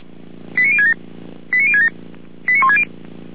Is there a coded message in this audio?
Yes: five-tone selective call (ZVEI) from 0.6 to 2.8 s, heard as quick runs of five tones, tones of about 70 ms. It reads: 89067 89067 89160